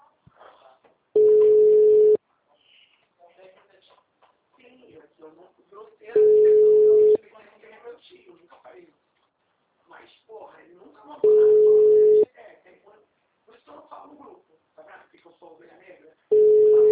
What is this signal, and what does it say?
Ringback tone (Europe, 425 Hz), with speech.